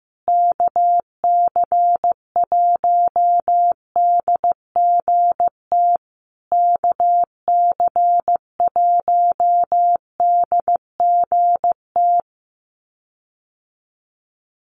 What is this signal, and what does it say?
Morse code, heard as a keyed tone, about 15 wpm, 700 Hz. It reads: KC1DGT KC1DGT
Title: USA Call Signs Repeated 3x 15wpm